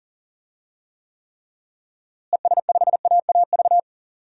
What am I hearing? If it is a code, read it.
Morse code, heard as a keyed tone, about 40 wpm, 700 Hz. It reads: ES5AAV
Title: DXCC-100 Call Signs - 6 Characters - Repeated 3x - 40wpm